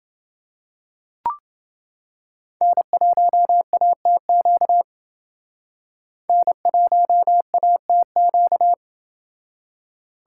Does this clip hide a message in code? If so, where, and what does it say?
Yes: Morse code from 2.6 to 8.7 s, heard as a keyed tone, about 29 wpm, 700 Hz. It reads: N1ATQ N1ATQ